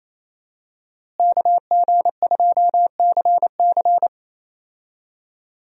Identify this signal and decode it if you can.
Morse code, heard as a keyed tone, about 27 wpm, 700 Hz. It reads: KG2CC
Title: USA Call Signs - 5 Characters - Repeated 2x - 28wpm